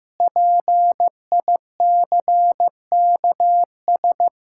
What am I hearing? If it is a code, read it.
Morse code, heard as a keyed tone, about 15 wpm, 700 Hz. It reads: PICKS
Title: Pangram Challenge v2 - 15wpm